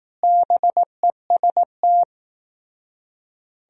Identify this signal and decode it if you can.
Morse code, heard as a keyed tone, about 18 wpm, 700 Hz. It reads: BEST
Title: Speed-Racing - Sets of 4 Words 15wpm